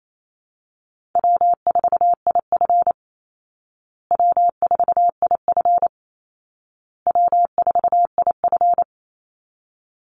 Morse code, heard as a keyed tone, about 28 wpm, 700 Hz. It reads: W4IF W4IF W4IF
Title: Easy Call Signs Repeated 3x - 28wpm